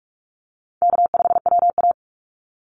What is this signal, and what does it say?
Morse code, heard as a keyed tone, about 44 wpm, 700 Hz. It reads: K5WA
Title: Call Signs Repeated 2x - 45wpm